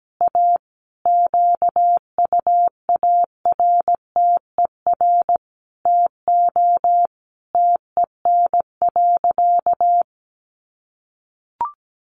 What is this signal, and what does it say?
Morse code, heard as a keyed tone, about 17 wpm, 700 Hz. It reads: A QUARTER TO TEN.
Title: Sentences from Top 1300 Words 17wpm